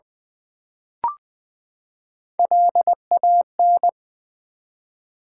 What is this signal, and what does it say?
Morse code, heard as a keyed tone, about 20 wpm, 700 Hz. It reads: LAN
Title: Build A Word - 3 Letters At A Time - 8-Letter Words - 20wpm